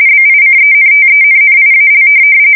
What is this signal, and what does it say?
RTTY, about 45.45 baud. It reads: BICENTENNIAL
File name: fsk170 / USS-New-Jersey